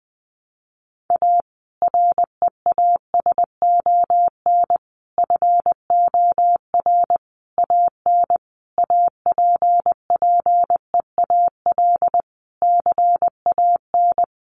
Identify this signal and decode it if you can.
Morse code, heard as a keyed tone, about 20 wpm, 700 Hz. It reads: A REASON FOR AN APPEAL CAN
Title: Sentences from Top 1300 Words 20wpm